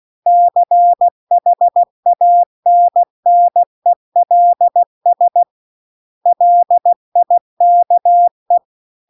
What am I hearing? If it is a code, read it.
Morse code, heard as a keyed tone, about 16 wpm, 700 Hz. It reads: CHANNELS LIKE
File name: CW-The-Big-Twoheaded-River-2-16WPM